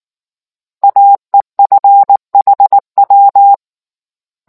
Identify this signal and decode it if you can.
Morse code, heard as a keyed tone, about 19 wpm, 800 Hz. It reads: AEFHW